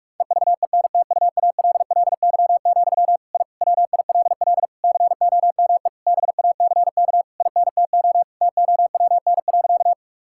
Morse code, heard as a keyed tone, about 46 wpm, 700 Hz. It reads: EVENTUALLY, I WILL COME BACK INTO TOWN.
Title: Sentences from Top 1000 Words - Review Part 6 - 45wpm